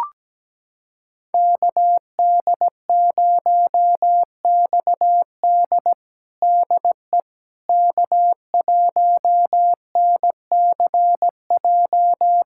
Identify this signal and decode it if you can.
Morse code, heard as a keyed tone, about 17 wpm, 700 Hz. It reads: KD0XD DE K1NCJ
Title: USA Callsign DE Callsign - 5 Characters - 17wpm